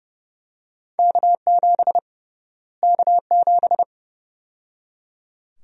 Morse code, heard as a keyed tone, about 30 wpm, 700 Hz. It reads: K7 K7
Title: USA Call Sign Prefixes - Repeated 2x - 30wpm